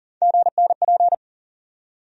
Morse code, heard as a keyed tone, about 40 wpm, 700 Hz. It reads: GNP
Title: USA Call Sign Postfixes - 3 Letters 40wpm